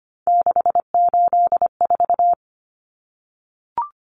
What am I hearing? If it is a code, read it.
Morse code, heard as a keyed tone, about 25 wpm, 700 Hz. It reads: 684
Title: Numbers - 3-Digit 25wpm